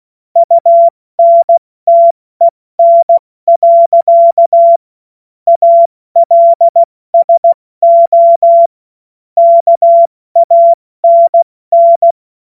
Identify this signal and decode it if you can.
Morse code, heard as a keyed tone, about 16 wpm, 680 Hz. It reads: UNTEN. ALSO KANN
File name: Morsekurs / Witze / Witz 21. Juni 2022 70 F80